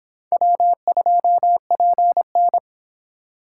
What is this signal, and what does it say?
Morse code, heard as a keyed tone, about 25 wpm, 700 Hz. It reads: W2PN